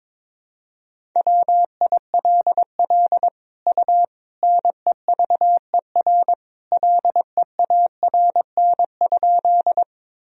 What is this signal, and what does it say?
Morse code, heard as a keyed tone, about 22 wpm, 700 Hz. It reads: WILL U NEVER LEARN?
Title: Sentences from Top 300 Words and Abbreviations 22wpm